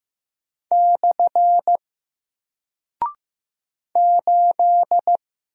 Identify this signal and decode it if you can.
Morse code, heard as a keyed tone, about 15 wpm, 700 Hz. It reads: / 8
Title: Single Character (Letter:Number:Punctuation:Prosign) Alt 15wpm